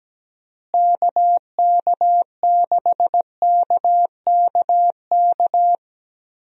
Morse code, heard as a keyed tone, about 17 wpm, 700 Hz. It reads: KK6KKK
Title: USA Call Signs - 6 Characters - Repeated 3x - 17wpm